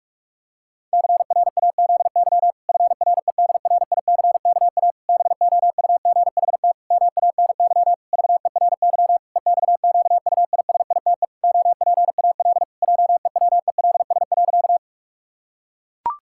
Morse code, heard as a keyed tone, about 43 wpm, 700 Hz. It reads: CRAZY FREDRICKA BOUGHT MANY VERY EXQUISITE OPAL JEWELS.
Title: Pangram Challenge v2 - 45wpm